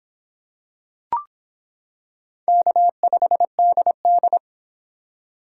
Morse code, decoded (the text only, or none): K5DD